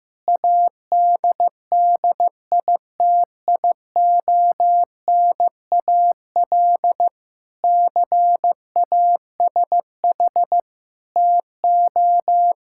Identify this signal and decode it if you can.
Morse code, heard as a keyed tone, about 15 wpm, 700 Hz. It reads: ADDITIONAL CASH TO
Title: Sentences from Top 1000 Words - Review Part 6 - 15wpm